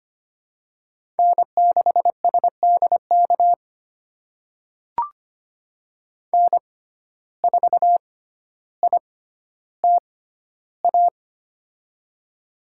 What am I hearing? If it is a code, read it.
Morse code, heard as a keyed tone, about 25 wpm, spaced as for 13 wpm, 700 Hz. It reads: N6SDK N4ITA
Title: USA Call Signs - 5 Characters - 6x Character Spacing - 25wpm